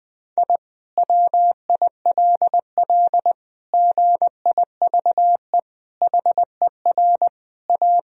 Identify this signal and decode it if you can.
Morse code, heard as a keyed tone, about 20 wpm, 700 Hz. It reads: I WILL GIVE HER A
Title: Sentences from Top 1000 Words - Review Part 2 - 20wpm